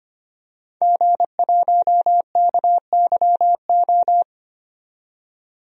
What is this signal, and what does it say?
Morse code, heard as a keyed tone, about 25 wpm, 700 Hz. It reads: G1KYO
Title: DXCC-100 Call Signs - 5 Characters - 25wpm